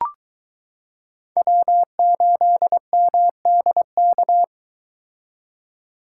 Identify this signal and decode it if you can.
Morse code, heard as a keyed tone, about 23 wpm, 700 Hz. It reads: W8MDK